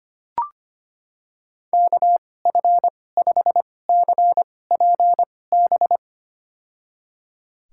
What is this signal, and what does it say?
Morse code, heard as a keyed tone, about 25 wpm, spaced as for 18 wpm, 700 Hz. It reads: KF5CPB
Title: USA Call Signs - 6 Characters - 2x Character Spacing - 25wpm